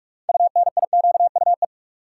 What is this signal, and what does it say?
Morse code, heard as a keyed tone, about 46 wpm, 700 Hz. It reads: UNIQUE
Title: Six Letter Words - 45wpm